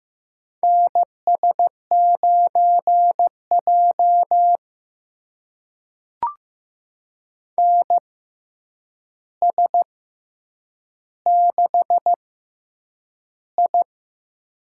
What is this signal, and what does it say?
Morse code, heard as a keyed tone, about 15 wpm, spaced as for 8 wpm, 700 Hz. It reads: NS9J NS6I
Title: USA Call Signs - 4 Characters - 6x Character Spacing - 15wpm